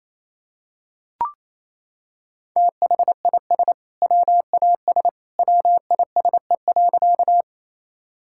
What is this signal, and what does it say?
Morse code, heard as a keyed tone, about 28 wpm, 700 Hz. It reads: THIS WAS WISE.
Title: Sentences from Top 1600 Words 28wpm